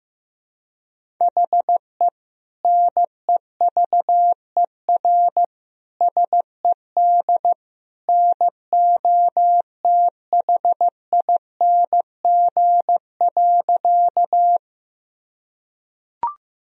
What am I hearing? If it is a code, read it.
Morse code, heard as a keyed tone, about 15 wpm, 700 Hz. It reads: HE NEVER SED NOTHING.
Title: Sentences from Top 500 Words and Abbreviations 15wpm